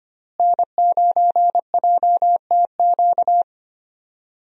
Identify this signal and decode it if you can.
Morse code, heard as a keyed tone, about 25 wpm, 700 Hz. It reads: N9JTQ
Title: USA Call Signs - 5 Characters - Repeated 3x - 25wpm